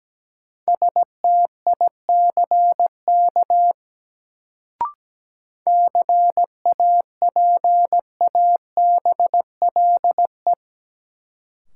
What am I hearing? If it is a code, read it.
Morse code, heard as a keyed tone, about 17 wpm, 700 Hz. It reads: STICK CAPABLE
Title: Sentences from Top 1200 Words - New Word Drill - 17wpm